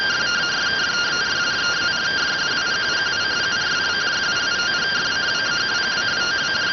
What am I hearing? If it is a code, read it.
Two-tone teleprinter signal, about 170 Hz shift.